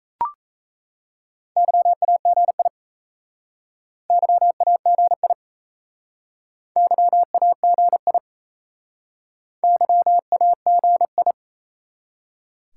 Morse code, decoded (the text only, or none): YAGI YAGI YAGI YAGI